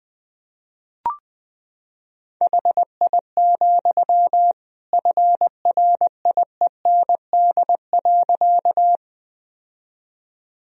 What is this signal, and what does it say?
Morse code, heard as a keyed tone, about 21 wpm, 700 Hz. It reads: HI, FRIEND.
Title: Sentences from Top 1000 Words - Review Part 3 - 20wpm